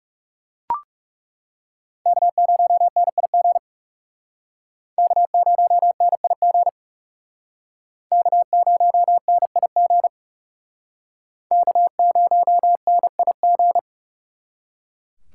Morse code, heard as a keyed tone, about 38 wpm, 700 Hz. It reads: K0NIG K0NIG K0NIG K0NIG